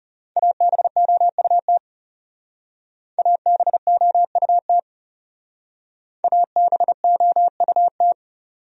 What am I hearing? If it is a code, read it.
Morse code, heard as a keyed tone, about 34 wpm, 700 Hz. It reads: ABOUT ABOUT ABOUT
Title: Speed-Racing Top 100 Words 30wpm